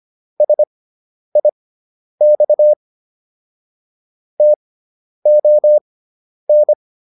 Morse code, heard as a keyed tone, about 25 wpm, spaced as for 10 wpm, 600 Hz. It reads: SIX TON